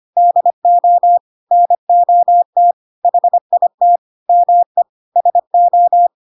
Morse code, heard as a keyed tone, about 25 wpm, 700 Hz. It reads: DO NOT HIT ME SO
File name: Pinocchio / Pinocchio ch1 25 25 ik1hge